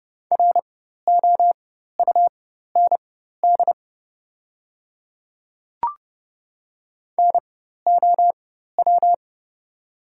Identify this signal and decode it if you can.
Morse code, heard as a keyed tone, about 30 wpm, spaced as for 14 wpm, 700 Hz. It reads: ROUND NOW
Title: 4x Character Spacing - Top 200 Words - 30wpm